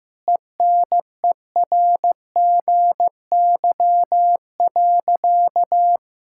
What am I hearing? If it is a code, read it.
Morse code, heard as a keyed tone, about 15 wpm, 700 Hz. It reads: ENERGY.
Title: Sentences from Top 1000 Words - Review Part 6 - 15wpm